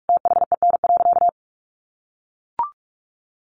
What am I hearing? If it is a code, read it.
Morse code, heard as a keyed tone, about 44 wpm, 700 Hz. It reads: THEN.
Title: Sentences from Top 200 Words 45wpm